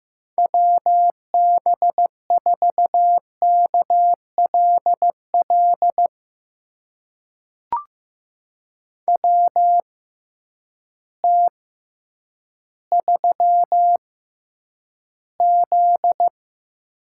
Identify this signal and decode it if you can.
Morse code, heard as a keyed tone, about 15 wpm, spaced as for 9 wpm, 700 Hz. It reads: WB4KLL WT3Z
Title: USA Call Signs - 6x Character Spacing - 15wpm